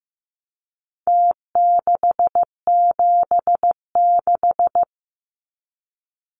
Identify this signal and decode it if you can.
Morse code, heard as a keyed tone, about 15 wpm, 700 Hz. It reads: T676
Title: Numbers - Contest Style - 4-Digit 15wpm